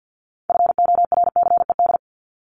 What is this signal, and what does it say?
Morse code, heard as a keyed tone, about 50 wpm, 700 Hz. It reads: FORCED
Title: ED Suffix - Top 100 Words with Base Words 50wpm